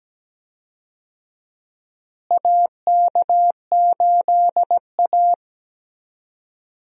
Morse code, heard as a keyed tone, about 17 wpm, 700 Hz. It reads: AK8A
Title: USA Call Signs - 4 Characters - Repeated 3x - 17wpm